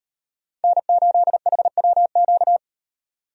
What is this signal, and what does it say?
Morse code, heard as a keyed tone, about 37 wpm, 700 Hz. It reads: N8HWQ